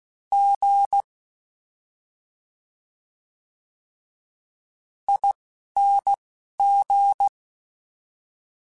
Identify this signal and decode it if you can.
Morse code, heard as a keyed tone, about 16 wpm, 800 Hz. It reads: G I N G
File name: Special INGs-mp3s / Special INGs-mp3s / Slow ing @10